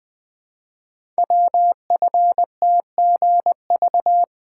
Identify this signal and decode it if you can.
Morse code, heard as a keyed tone, about 20 wpm, 700 Hz. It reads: WFTGV
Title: Code Groups - 5 Letters - 20wpm